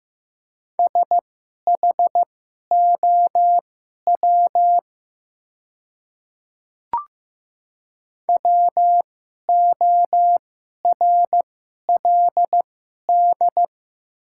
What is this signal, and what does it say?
Morse code, heard as a keyed tone, about 15 wpm, spaced as for 11 wpm, 700 Hz. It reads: SHOW WORLD
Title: 2x Character Spacing - Top 200 Words - 15wpm